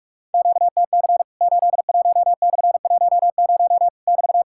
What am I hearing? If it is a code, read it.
Morse code, heard as a keyed tone, about 46 wpm, 700 Hz. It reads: QTC 81/10 <BT>
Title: Worked All Europe DX Contest - QTC Exchange - 45wpm